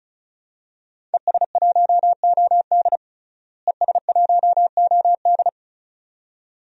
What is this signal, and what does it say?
Morse code, heard as a keyed tone, about 34 wpm, 700 Hz. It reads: ES1OD ES1OD
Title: DXCC-100 Call Signs - 5 Characters - Repeated 2x - 35wpm